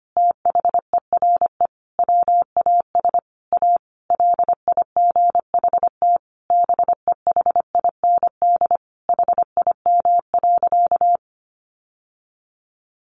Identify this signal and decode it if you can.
Morse code, heard as a keyed tone, about 25 wpm, 700 Hz. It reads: THERE WAS A LIGHT BEHIND HIM.
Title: Sentences from Top 1000 Words 25wpm